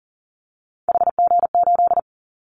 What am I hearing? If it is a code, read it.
Morse code, heard as a keyed tone, about 40 wpm, 700 Hz. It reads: HG8
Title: Callsign Prefixes for Top 25 DXCC - 40wpm